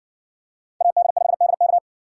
Morse code, heard as a keyed tone, about 54 wpm, 700 Hz. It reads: AD5DK